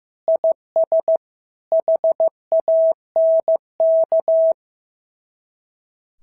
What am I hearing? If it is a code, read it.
Morse code, heard as a keyed tone, about 14 wpm, 650 Hz. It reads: IS HANK
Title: SOTA QSO Practice 15wpm